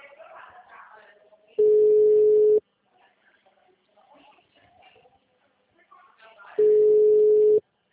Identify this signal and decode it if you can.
Ringback tone (Europe, 425 Hz), with speech.
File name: out- -2009- - - .225